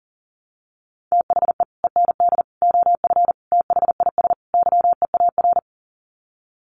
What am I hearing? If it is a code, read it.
Morse code, heard as a keyed tone, about 40 wpm, 700 Hz. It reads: THE END OF THIS YEAR
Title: Sets of 5 Words Encore 40wpm